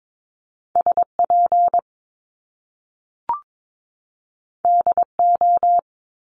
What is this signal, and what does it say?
Morse code, heard as a keyed tone, about 22 wpm, 700 Hz. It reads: SP DO
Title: Top Two Letter Combinations in Words 22wpm